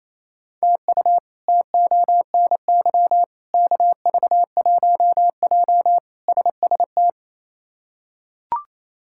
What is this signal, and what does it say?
Morse code, heard as a keyed tone, about 28 wpm, 700 Hz. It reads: TU TONY KV1J SST
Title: K1USN SST QSO Practice - 28wpm